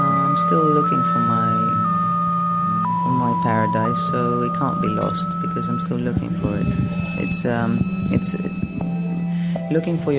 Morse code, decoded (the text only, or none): none